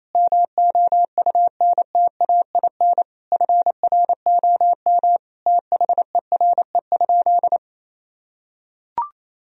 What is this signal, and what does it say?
Morse code, heard as a keyed tone, about 28 wpm, 700 Hz. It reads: MOUNTAIN FROM THERE?
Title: Sentences from Top 1000 Words - Review Part 1 - 28wpm